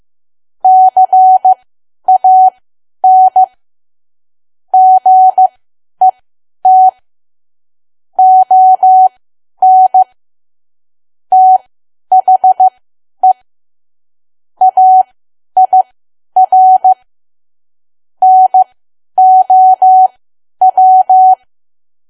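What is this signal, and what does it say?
Morse code, heard as a keyed tone, about 15 wpm, spaced as for 10 wpm, 750 Hz. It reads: CAN GET ON THE AIR NOW
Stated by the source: Morse learning practice at 10 wpm.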